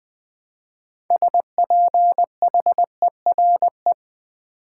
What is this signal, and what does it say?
Morse code, heard as a keyed tone, about 20 wpm, 700 Hz. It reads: SPHERE
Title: 6 Letter Words - First 3 Letters Given - 20wpm